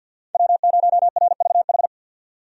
Morse code, heard as a keyed tone, about 48 wpm, 700 Hz. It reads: W0RVH
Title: USA Call Signs - 5 Characters - 50wpm